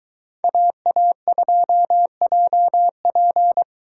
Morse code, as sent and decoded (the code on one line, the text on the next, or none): .- .- ..--- .--- .--.
AA2JP